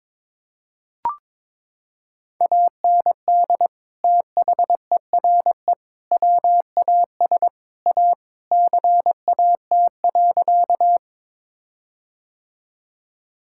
Morse code, heard as a keyed tone, about 22 wpm, 700 Hz. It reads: AND THERE WAS A CAT.
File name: practice / 22wpm / Sentences with 4-Word Phrases - 22wpm